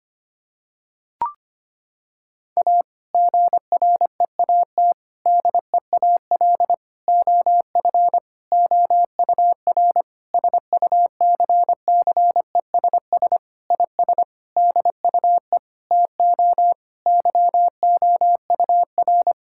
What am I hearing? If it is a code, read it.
Morse code, heard as a keyed tone, about 25 wpm, 700 Hz. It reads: A GREAT DEAL OF OUR SUCCESS IS DUE TO YOUR
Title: Sentences from Top 1000 Words - Review Part 6 - 25wpm